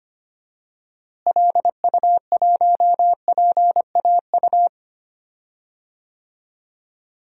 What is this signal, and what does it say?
Morse code, heard as a keyed tone, about 25 wpm, 700 Hz. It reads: LU1PAU
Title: DXCC-100 Call Signs - 6 Characters - Repeated 2x - 25wpm